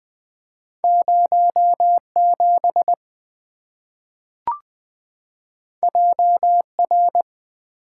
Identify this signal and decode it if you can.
Morse code, heard as a keyed tone, about 20 wpm, 700 Hz. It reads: 07 JR
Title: Two Letter-Number Combinations - 20wpm